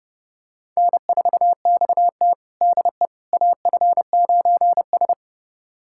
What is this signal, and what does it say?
Morse code, heard as a keyed tone, about 30 wpm, 700 Hz. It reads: N4XT DE AF9S
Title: USA Callsign DE Callsign - 4 Characters - 30wpm